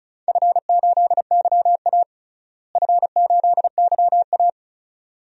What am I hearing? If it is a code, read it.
Morse code, heard as a keyed tone, about 36 wpm, 700 Hz. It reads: F8YA F8YA
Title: DXCC-100 Call Signs - 4 Characters - Repeated 2x - 35wpm